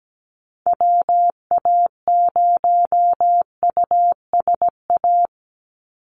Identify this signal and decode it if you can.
Morse code, heard as a keyed tone, about 17 wpm, 700 Hz. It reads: WA0USA
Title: Call Signs Repeated 3x - 17wpm